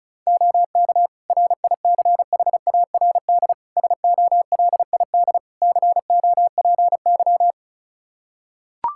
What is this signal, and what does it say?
Morse code, heard as a keyed tone, about 36 wpm, 700 Hz. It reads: OK RICHARD SOLID COPY
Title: Practice QSO - Exchange 3 - More Info - 35wpm